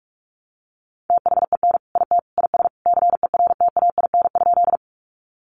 Morse code, heard as a keyed tone, about 44 wpm, 700 Hz. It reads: THEN IT IS CERTAIN?
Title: Sentences from Top 500 Words 45wpm